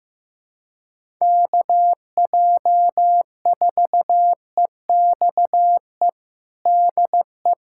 Morse code, heard as a keyed tone, about 15 wpm, 700 Hz. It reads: KJ4EXE DE
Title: USA Callsign DE Callsign - 6 Characters - 15wpm